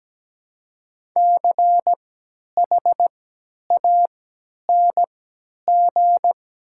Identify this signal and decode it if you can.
Morse code, heard as a keyed tone, about 17 wpm, spaced as for 10 wpm, 700 Hz. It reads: CHANG
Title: 3x Character Spacing - Top 200 Words - 17wpm